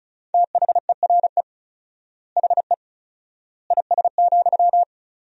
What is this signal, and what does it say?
Morse code, heard as a keyed tone, about 34 wpm, 700 Hz. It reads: THERE HE IS,
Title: 4x Word-Spacing - Sentences from Top 100 Words - 35wpm